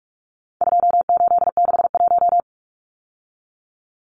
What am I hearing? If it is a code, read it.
Morse code, heard as a keyed tone, about 44 wpm, 700 Hz. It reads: 2861
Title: Numbers - 4-Digit 45wpm